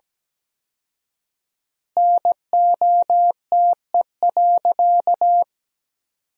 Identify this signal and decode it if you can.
Morse code, heard as a keyed tone, about 17 wpm, 700 Hz. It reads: NOTE.
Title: 4x Word-Spacing - Sentences from Top 500 Words - 17wpm